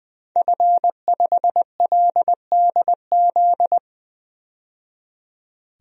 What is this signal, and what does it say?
Morse code, heard as a keyed tone, about 20 wpm, 700 Hz. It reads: F5LDZ
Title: DXCC-100 Call Signs - 5 Characters - 20wpm